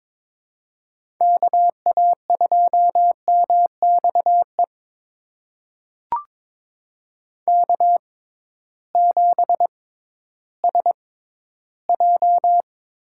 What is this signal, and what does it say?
Morse code, heard as a keyed tone, about 22 wpm, spaced as for 13 wpm, 700 Hz. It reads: KA2MXE K7SJ
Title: USA Call Signs - 6x Character Spacing - 22wpm